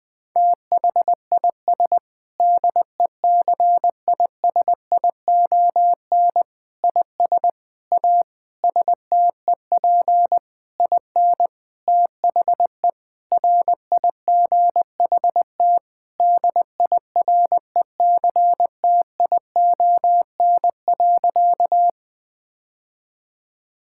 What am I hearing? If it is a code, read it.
Morse code, heard as a keyed tone, about 20 wpm, 700 Hz. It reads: THIS DECISION IS A STEP IN THE RIGHT DIRECTION.
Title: Sentences with 7-Word Phrases - 20wpm